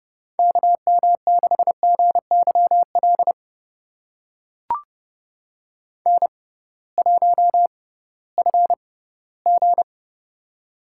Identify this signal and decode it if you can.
Morse code, heard as a keyed tone, about 30 wpm, spaced as for 17 wpm, 700 Hz. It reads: KM6GYL N1FG
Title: USA Call Signs - 6x Character Spacing - 30wpm